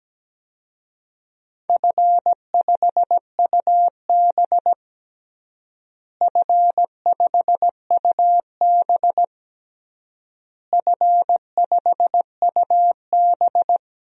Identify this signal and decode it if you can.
Morse code, heard as a keyed tone, about 17 wpm, 700 Hz. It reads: F5UB F5UB F5UB
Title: DXCC-100 Call Signs - 4 Characters - Repeated 3x - 17wpm